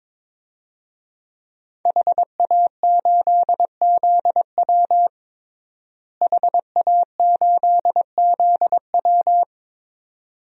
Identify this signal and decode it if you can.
Morse code, heard as a keyed tone, about 22 wpm, 700 Hz. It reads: HA8ZW HA8ZW
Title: DXCC-100 Call Signs - 5 Characters - Repeated 2x - 22wpm